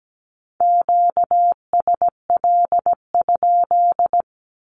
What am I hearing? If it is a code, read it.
Morse code, heard as a keyed tone, about 17 wpm, 700 Hz. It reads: QSL?
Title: Q-Codes with Spoken Meaning - 17wpm